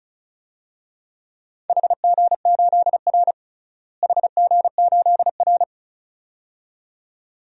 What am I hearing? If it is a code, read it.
Morse code, heard as a keyed tone, about 36 wpm, 700 Hz. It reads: HG8R HG8R
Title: DXCC-100 Call Signs - 4 Characters - Repeated 2x - 35wpm